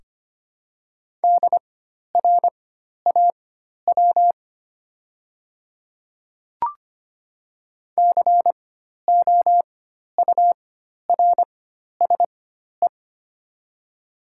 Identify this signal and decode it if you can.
Morse code, heard as a keyed tone, about 25 wpm, spaced as for 12 wpm, 700 Hz. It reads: DRAW COURSE
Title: 4x Character Spacing - Top 500 Words - 25wpm